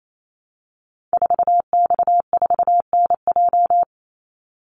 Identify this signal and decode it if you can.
Morse code, heard as a keyed tone, about 28 wpm, 700 Hz. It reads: 4X4NJ
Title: Call Signs - 28wpm